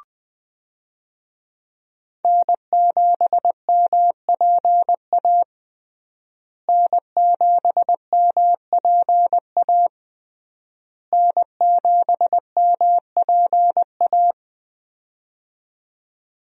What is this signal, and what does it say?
Morse code, heard as a keyed tone, about 20 wpm, 700 Hz. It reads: N7MPA N7MPA N7MPA
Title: USA Call Signs - 5 Characters - Repeated 3x - 20wpm